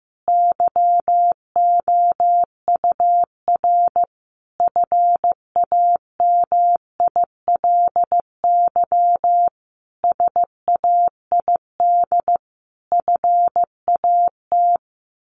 Morse code, heard as a keyed tone, about 15 wpm, 700 Hz. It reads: YOUR FAMILY SAID FAT
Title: Sentences from Top 1100 Words 15wpm